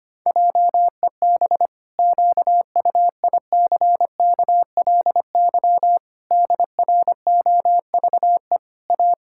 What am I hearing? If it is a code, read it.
Morse code, heard as a keyed tone, about 25 wpm, 700 Hz. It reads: JEB QUICKLY DROVE A
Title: Pangram Challenge v2 - 25wpm